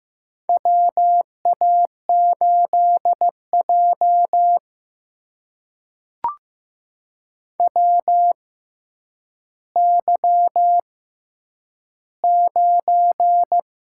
Morse code, heard as a keyed tone, about 15 wpm, spaced as for 9 wpm, 700 Hz. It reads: WA8J WY9
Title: USA Call Signs - 6x Character Spacing - 15wpm